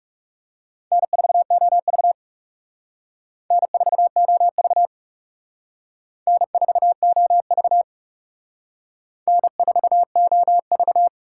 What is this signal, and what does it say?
Morse code, heard as a keyed tone, about 37 wpm, 700 Hz. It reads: N4OV N4OV N4OV N4OV
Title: Speed Racing - USA Call Signs 30wpm